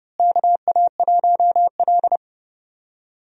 Morse code, heard as a keyed tone, about 30 wpm, 700 Hz. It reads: KA1L
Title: USA Call Signs - 4 Characters - Repeated 2x - 30wpm